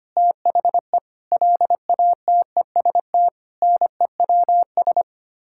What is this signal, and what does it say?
Morse code, heard as a keyed tone, about 25 wpm, 700 Hz. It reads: THE LATEST NEWS
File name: Sets of 9 Words - 25wpm